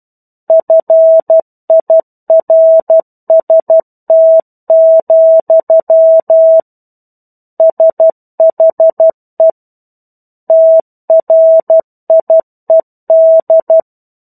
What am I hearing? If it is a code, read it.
Morse code, heard as a keyed tone, about 12 wpm, 650 Hz. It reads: FIRST, SHE TRIED
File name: 01-Chapter-01 12 WPM